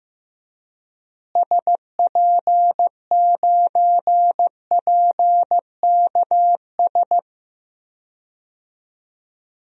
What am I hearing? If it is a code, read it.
Morse code, heard as a keyed tone, about 15 wpm, 700 Hz. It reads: SP9PKS
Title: DXCC-100 Call Signs - 6 Characters - Repeated 2x - 15wpm